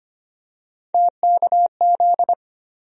Morse code, heard as a keyed tone, about 25 wpm, 700 Hz. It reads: TKZ
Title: USA Call Sign Postfixes - 3 Letters 25wpm